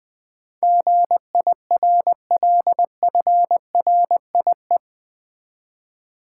Morse code, heard as a keyed tone, about 20 wpm, 700 Hz. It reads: GIRLFRIE
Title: Build A Word - 4 Letters At A Time - 10-Letter Words - 20wpm